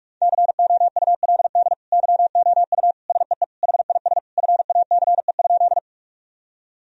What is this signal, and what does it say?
Morse code, heard as a keyed tone, about 46 wpm, 700 Hz. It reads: COULD YOU SEE HIS FACE?
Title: Sentences from Top 1000 Words - Review Part 2 - 45wpm